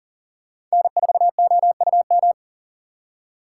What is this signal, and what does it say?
Morse code, heard as a keyed tone, about 40 wpm, 700 Hz. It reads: N4OUM
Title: USA Call Signs - 5 Characters - 3x Character Spacing - 40wpm